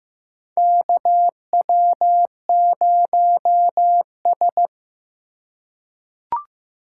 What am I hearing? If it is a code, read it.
Morse code, heard as a keyed tone, about 15 wpm, 700 Hz. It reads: KW0S
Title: USA Call Signs - 4 Characters - 2x Character Spacing - 15wpm